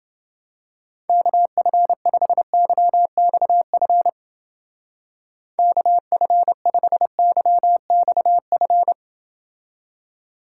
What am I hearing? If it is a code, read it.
Morse code, heard as a keyed tone, about 29 wpm, 700 Hz. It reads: KF5YXF KF5YXF